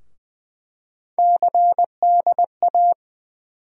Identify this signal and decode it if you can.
Morse code, heard as a keyed tone, about 20 wpm, 700 Hz. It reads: CDA
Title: USA Call Sign Postfixes - 3 Letters 20wpm